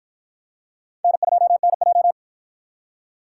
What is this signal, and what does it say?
Morse code, heard as a keyed tone, about 53 wpm, 700 Hz. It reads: N2NJ